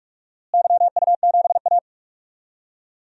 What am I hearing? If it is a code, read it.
Morse code, heard as a keyed tone, about 44 wpm, 700 Hz. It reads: YU7A
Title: DXCC-100 Call Signs - 4 Characters - Repeated 2x - 45wpm